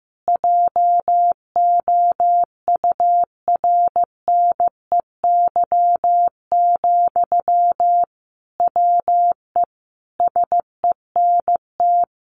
Morse code, heard as a keyed tone, about 15 wpm, 700 Hz. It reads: JOURNEY, WE SENT
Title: A Sound Of Thunder 15wpm